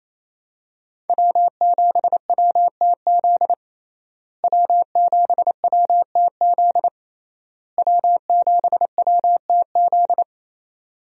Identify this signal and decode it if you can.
Morse code, heard as a keyed tone, about 27 wpm, 700 Hz. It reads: W7WTZ W7WTZ W7WTZ
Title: USA Call Signs - 5 Characters - Repeated 3x - 28wpm